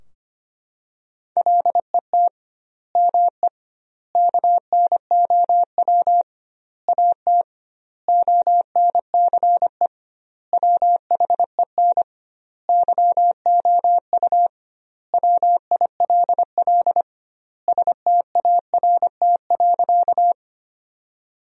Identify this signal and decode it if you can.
Morse code, heard as a keyed tone, about 25 wpm, 700 Hz. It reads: LET ME KNOW AT ONCE WHEN YOU WILL START.
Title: 2x Word-Spacing - Sentences from Top 300 Words - 25wpm